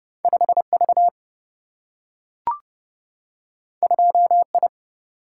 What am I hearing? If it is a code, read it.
Morse code, heard as a keyed tone, about 30 wpm, 700 Hz. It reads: 5V 2I
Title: Single-Letter Single-Number 30wpm